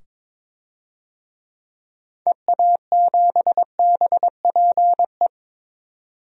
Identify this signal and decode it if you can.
Morse code, heard as a keyed tone, about 21 wpm, 700 Hz. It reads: EA7BPE
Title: DXCC-100 Call Signs - 6 Characters - 22wpm